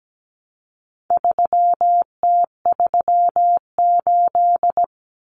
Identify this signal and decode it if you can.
Morse code, heard as a keyed tone, about 17 wpm, 700 Hz. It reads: 3T38
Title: Common Cut Numbers - 4-Digit 17wpm